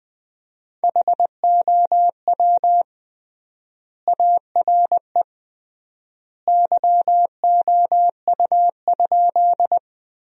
Morse code, heard as a keyed tone, about 21 wpm, 700 Hz. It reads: HOW ARE YOU?
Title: 3x Word-Spacing - Sentences from Top 100 Words - 20wpm